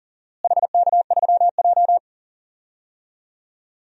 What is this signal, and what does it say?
Morse code, heard as a keyed tone, about 40 wpm, 700 Hz. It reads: HK3J
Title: DXCC-100 Call Signs - 4 Characters - Repeated 3x - 40wpm